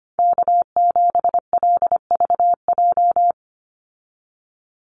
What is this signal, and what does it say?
Morse code, heard as a keyed tone, about 25 wpm, 700 Hz. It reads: K7LVJ
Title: Call Signs Repeated 2x - 25wpm